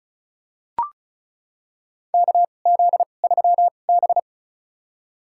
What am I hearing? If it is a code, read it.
Morse code, heard as a keyed tone, about 36 wpm, spaced as for 25 wpm, 700 Hz. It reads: KZ3B
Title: USA Call Signs - 4 Characters - 2x Character Spacing - 35wpm